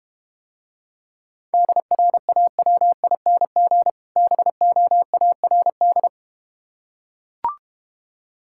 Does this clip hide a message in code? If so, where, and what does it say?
Yes: Morse code from 1.5 to 6.1 s, heard as a keyed tone, about 31 wpm, 700 Hz. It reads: DRAWING BOARD